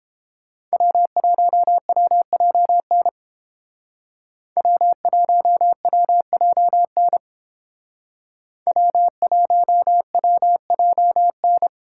Morse code, heard as a keyed tone, about 29 wpm, 700 Hz. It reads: W1WJN W1WJN W1WJN